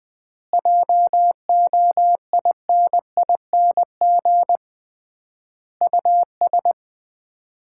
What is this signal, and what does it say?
Morse code, heard as a keyed tone, about 21 wpm, 700 Hz. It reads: JOINING US
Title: 3x Word-Spacing - Sets of 5 Words - 20wpm